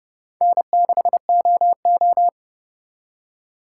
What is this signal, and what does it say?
Morse code, heard as a keyed tone, about 40 wpm, 700 Hz. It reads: N6OO